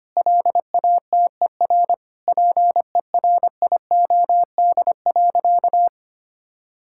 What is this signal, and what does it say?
Morse code, heard as a keyed tone, about 25 wpm, 700 Hz. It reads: LATER PERIOD.
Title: Sentences from Top 1000 Words - Review Part 5 - 25wpm